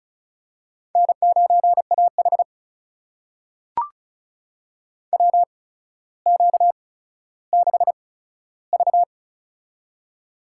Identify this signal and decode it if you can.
Morse code, heard as a keyed tone, about 36 wpm, spaced as for 15 wpm, 700 Hz. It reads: N9AH WQ6V
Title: USA Call Signs - 4 Characters - 8x Character Spacing - 35wpm